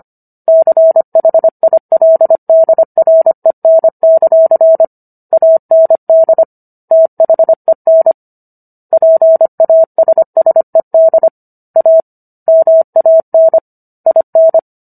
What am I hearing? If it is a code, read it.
Morse code, heard as a keyed tone, about 25 wpm, 650 Hz. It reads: CHILDREN; AND THEN PASSED A MAN IN